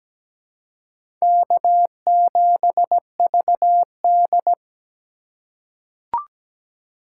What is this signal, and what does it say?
Morse code, heard as a keyed tone, about 17 wpm, 700 Hz. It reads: K7VD
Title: USA Call Signs - 4 Characters - 4x Character Spacing - 17wpm